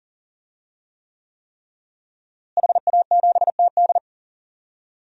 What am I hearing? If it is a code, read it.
Morse code, heard as a keyed tone, about 38 wpm, 700 Hz. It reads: HA7TD
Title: DXCC-100 Call Signs - 5 Characters - Repeated 3x - 40wpm